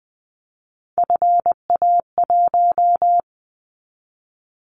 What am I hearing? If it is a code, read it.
Morse code, heard as a keyed tone, about 20 wpm, 700 Hz. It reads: FA1
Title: Callsign Prefixes for Top 50 DXCC - 20wpm